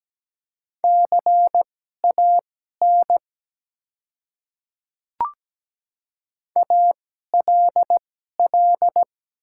Morse code, heard as a keyed tone, about 17 wpm, spaced as for 12 wpm, 700 Hz. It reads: CAN ALL
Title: 2x Character Spacing - Top 100 Words - 17wpm